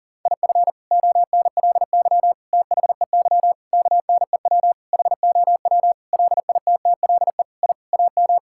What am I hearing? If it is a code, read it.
Morse code, heard as a keyed tone, about 40 wpm, 700 Hz. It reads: IF ONLY THEY KNEW HOW LITTLE I AM
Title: Sentences from Top 1000 Words - Review Part 6 - 40wpm